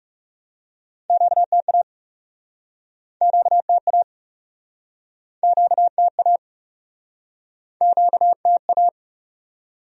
Morse code, heard as a keyed tone, about 41 wpm, 700 Hz. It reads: QTA QTA QTA QTA